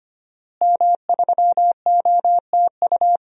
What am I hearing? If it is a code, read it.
Morse code, heard as a keyed tone, about 25 wpm, 700 Hz. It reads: M3OTU
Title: DXCC-100 Call Signs - 5 Characters - Repeated 3x - 25wpm